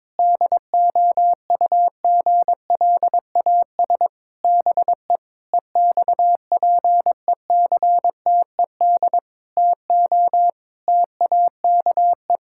Morse code, heard as a keyed tone, about 22 wpm, 700 Hz. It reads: DOUGLAS BE EXPECTED TO TAKE
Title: Pangram Challenge v2 - 22wpm